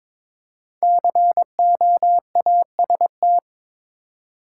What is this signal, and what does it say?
Morse code, heard as a keyed tone, about 22 wpm, 700 Hz. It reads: COAST
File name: 5 Letter Words - First 3 Letters Given - 22wpm